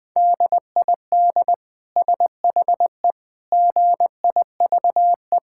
Morse code, heard as a keyed tone, about 20 wpm, 700 Hz. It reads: DID SHE GIVE
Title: Sentences from Top 1000 Words - Review Part 5 - 20wpm